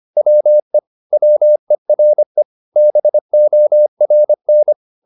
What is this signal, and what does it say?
Morse code, heard as a keyed tone, about 25 wpm, 600 Hz. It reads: WE WERE BORN